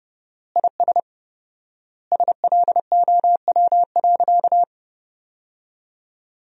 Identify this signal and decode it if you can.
Morse code, heard as a keyed tone, about 30 wpm, 700 Hz. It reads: IS SLOW.
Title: 4x Word-Spacing - Sentences from Top 400 Words - 30wpm